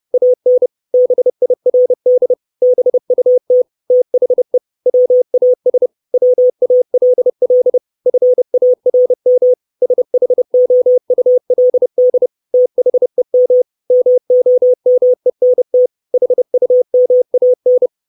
Morse code, heard as a keyed tone, about 30 wpm, 500 Hz. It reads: AN BIRD BUT THE WAS WALL FARM SHOULD THEM MOMENT HUMAN
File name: 632 Words 30WPM